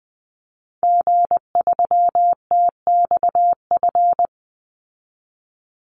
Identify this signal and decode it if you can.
Morse code, heard as a keyed tone, about 20 wpm, 700 Hz. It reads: G3TXF
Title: Call Signs Repeated 3x - 20wpm